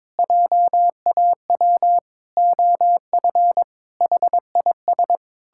Morse code, heard as a keyed tone, about 22 wpm, 700 Hz. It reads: JAW OF HIS
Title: Pangram Challenge v2 - 22wpm